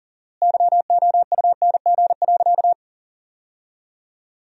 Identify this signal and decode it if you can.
Morse code, heard as a keyed tone, about 40 wpm, 700 Hz. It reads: YOUNG.
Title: 4x Word-Spacing - Sentences from Top 400 Words - 40wpm